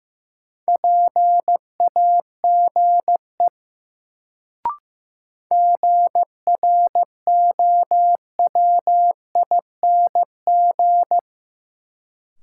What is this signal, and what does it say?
Morse code, heard as a keyed tone, about 16 wpm, 700 Hz. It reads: PAGE GROWING
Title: Sets of 5 Words Encore - New Word Drill - 15wpm